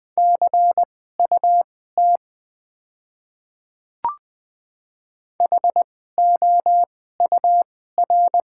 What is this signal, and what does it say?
Morse code, heard as a keyed tone, about 20 wpm, spaced as for 14 wpm, 700 Hz. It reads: CUT HOUR
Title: 2x Character Spacing - Top 500 Words - 20wpm